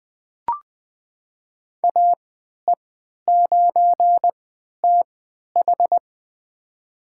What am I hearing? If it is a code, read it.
Morse code, heard as a keyed tone, about 20 wpm, spaced as for 11 wpm, 700 Hz. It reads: AE9TH
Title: USA Call Signs - 5 Characters - 3x Character Spacing - 20wpm